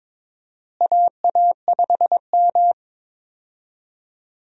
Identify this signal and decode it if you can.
Morse code, heard as a keyed tone, about 22 wpm, 700 Hz. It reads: AA5M
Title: USA Call Signs - 4 Characters - Repeated 2x - 22wpm